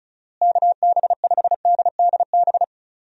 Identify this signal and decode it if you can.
Morse code, heard as a keyed tone, about 34 wpm, 700 Hz. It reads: KB5DDB
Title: USA Call Signs - 6 Characters - Repeated 2x - 35wpm